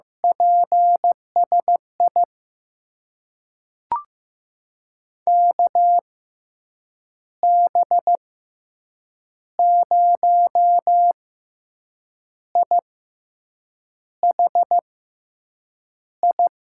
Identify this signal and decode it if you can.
Morse code, heard as a keyed tone, about 15 wpm, spaced as for 6 wpm, 700 Hz. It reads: PSI KB0IHI
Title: USA Call Signs - 6 Characters - 6x Character Spacing - 15wpm